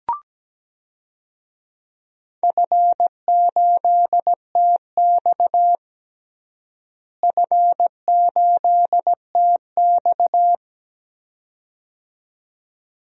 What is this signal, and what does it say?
Morse code, heard as a keyed tone, about 17 wpm, 700 Hz. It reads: F8TX F8TX
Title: DXCC-100 Call Signs - 4 Characters - Repeated 2x - 17wpm